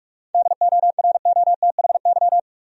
Morse code, heard as a keyed tone, about 46 wpm, 700 Hz. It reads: DOROTHY
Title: USA Common Names - 45wpm